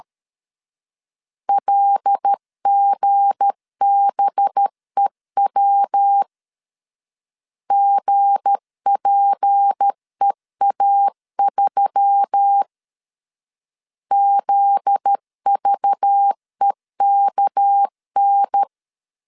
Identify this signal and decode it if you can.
Morse code, heard as a keyed tone, about 13 wpm, 800 Hz. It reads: LGBEW GPEA3 ZVEKN
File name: LEZIONI / 7 GF9Z / 7 GF9Z 12wpm 2147